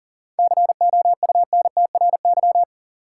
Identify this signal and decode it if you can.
Morse code, heard as a keyed tone, about 40 wpm, 700 Hz. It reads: COUNTRY
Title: Sentences from Top 300 Words - New Word Drill - 40wpm